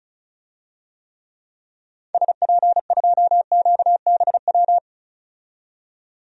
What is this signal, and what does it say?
Morse code, heard as a keyed tone, about 36 wpm, 700 Hz. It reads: SP2QBW
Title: DXCC-100 Call Signs - 6 Characters - 35wpm